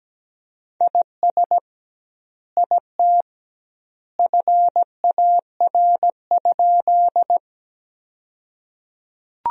Morse code, heard as a keyed tone, about 17 wpm, 700 Hz. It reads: IS IT FAR?
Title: 2x Word-Spacing - Sentences from Top 300 Words - 17wpm